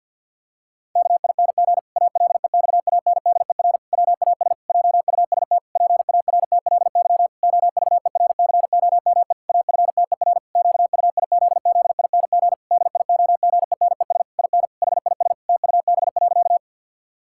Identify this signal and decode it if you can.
Morse code, heard as a keyed tone, about 50 wpm, 700 Hz. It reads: KING ALEXANDER WAS JUST PARTLY OVERCOME AFTER QUIZZING DIOGENES IN HIS TUB.
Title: Pangram Challenge v2 - 50wpm